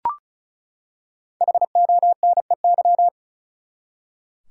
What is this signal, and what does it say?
Morse code, heard as a keyed tone, about 36 wpm, 700 Hz. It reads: HONEY
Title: Five Letter Words - 35wpm